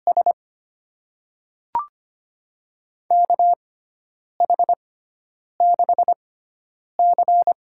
Morse code, heard as a keyed tone, about 25 wpm, spaced as for 9 wpm, 700 Hz. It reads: S KH6C
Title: USA Call Signs - 6 Characters - 6x Character Spacing - 25wpm